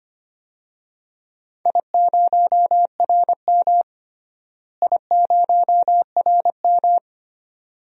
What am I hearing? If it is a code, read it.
Morse code, heard as a keyed tone, about 25 wpm, 700 Hz. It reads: I0RM I0RM
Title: DXCC-100 Call Signs - 4 Characters - Repeated 2x - 25wpm